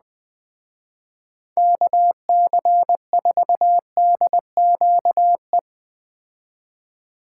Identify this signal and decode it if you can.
Morse code, heard as a keyed tone, about 20 wpm, 700 Hz. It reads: KC4DQE
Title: Speed Racing - USA 2x3 Call Signs 17wpm